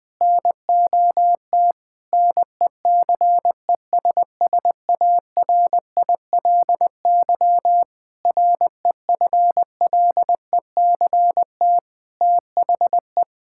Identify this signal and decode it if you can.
Morse code, heard as a keyed tone, about 20 wpm, 700 Hz. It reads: NOT NECESSARILY REFLECT THE
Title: Sentences with 8-Word Phrases - 20wpm